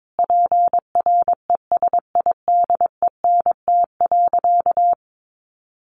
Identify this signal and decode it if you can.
Morse code, heard as a keyed tone, about 22 wpm, 700 Hz. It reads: PRESIDENT.
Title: Sentences from Top 1000 Words 22wpm